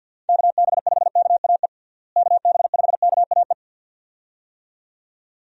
Morse code, heard as a keyed tone, about 48 wpm, 700 Hz. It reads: KB5KAE KB5KAE
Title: USA Call Signs - 6 Characters - Repeated 2x - 50wpm